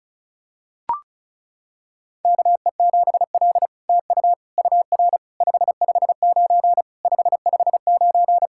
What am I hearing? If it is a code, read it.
Morse code, heard as a keyed tone, about 36 wpm, 700 Hz. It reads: KE7L TU UR 559 559
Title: SOTA QSO Practice 35wpm